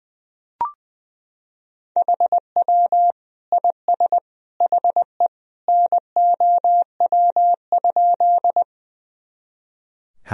Morse code, heard as a keyed tone, about 21 wpm, 700 Hz. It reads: HW IS HE NOW?
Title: Sentences from Top 100 Words and Abbreviations 20wpm